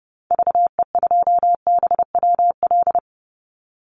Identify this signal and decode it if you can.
Morse code, heard as a keyed tone, about 31 wpm, 700 Hz. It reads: VE2BWL
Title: Call Signs - 30wpm